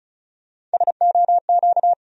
Morse code, heard as a keyed tone, about 34 wpm, 700 Hz. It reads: SOQ
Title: USA Call Sign Postfixes - 3 Letters 35wpm